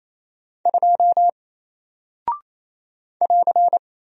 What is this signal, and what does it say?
Morse code, heard as a keyed tone, about 28 wpm, 700 Hz. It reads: 2 <AR>
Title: Single Character (Letter:Number:Punctuation:Prosign) Alt 28wpm